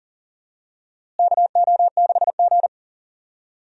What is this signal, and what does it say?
Morse code, heard as a keyed tone, about 40 wpm, 700 Hz. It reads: KO6G
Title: USA Call Signs - 4 Characters - Repeated 3x - 40wpm